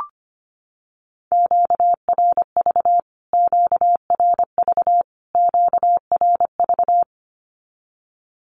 Morse code, heard as a keyed tone, about 25 wpm, 700 Hz. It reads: QRV QRV QRV
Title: Q-Codes Repeated 3x 25wpm